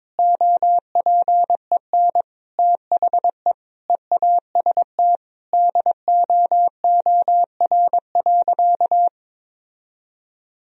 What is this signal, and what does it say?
Morse code, heard as a keyed tone, about 21 wpm, 700 Hz. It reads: OPEN THE EAST DOOR.
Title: Sentences from Top 1000 Words - Review Part 3 - 22wpm